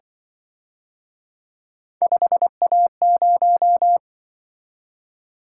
Morse code, heard as a keyed tone, about 24 wpm, 700 Hz. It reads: 5A0
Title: Speed-Racing - Three Letter-Number Combinations - 20wpm